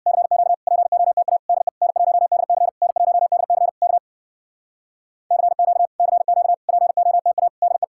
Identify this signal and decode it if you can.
Morse code, heard as a keyed tone, about 67 wpm, 700 Hz. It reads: CQ POTA DE N1DC N1DC K CQ CQ POTA DE